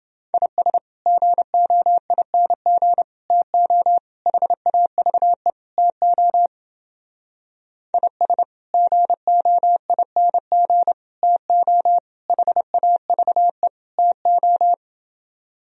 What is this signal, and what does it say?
Morse code, heard as a keyed tone, about 29 wpm, 700 Hz. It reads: IS GOING TO HAVE TO IS GOING TO HAVE TO